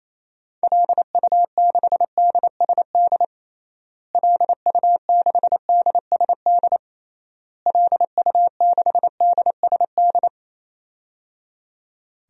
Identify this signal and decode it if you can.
Morse code, heard as a keyed tone, about 27 wpm, 700 Hz. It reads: LU6DSD LU6DSD LU6DSD
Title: DXCC-100 Call Signs - 6 Characters - Repeated 3x - 28wpm